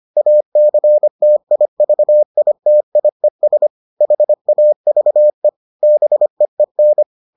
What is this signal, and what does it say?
Morse code, heard as a keyed tone, about 25 wpm, 600 Hz. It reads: ACTIVITIES HAVE BEEN